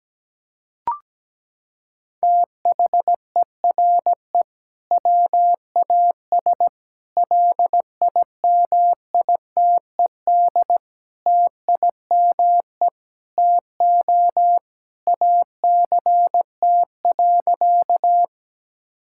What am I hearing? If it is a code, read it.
Morse code, heard as a keyed tone, about 17 wpm, 700 Hz. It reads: THERE WAS LIMITED TIME TO ACT.
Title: Sentences from Top 1000 Words - Review Part 6 - 17wpm